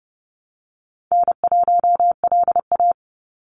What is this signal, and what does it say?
Morse code, heard as a keyed tone, about 30 wpm, 700 Hz. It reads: N1LA
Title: USA Call Signs 30wpm